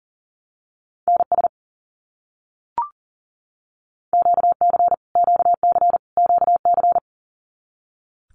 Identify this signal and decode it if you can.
Morse code, heard as a keyed tone, about 40 wpm, 700 Hz. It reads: NS QC QC QC
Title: Canadian Provinces and Territories - Repeated 3x - 40wpm